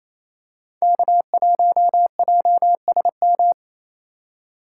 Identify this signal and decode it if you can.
Morse code, heard as a keyed tone, about 28 wpm, 700 Hz. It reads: K1JSM
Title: USA Call Signs - 5 Characters - 2x Character Spacing - 28wpm